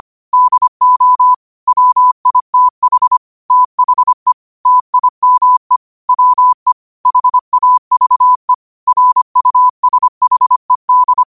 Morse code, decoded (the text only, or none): DO WITH THE TIME WE HAVE RUSHED